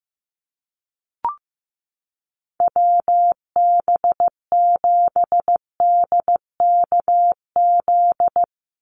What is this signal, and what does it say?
Morse code, heard as a keyed tone, about 15 wpm, 700 Hz. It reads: WB7DKZ
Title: Call Signs Repeated 3x - 15wpm